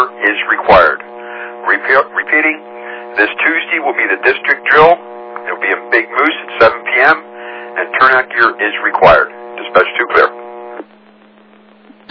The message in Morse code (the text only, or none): none